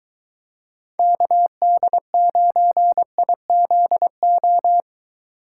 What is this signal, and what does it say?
Morse code, heard as a keyed tone, about 22 wpm, 700 Hz. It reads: KD9IZO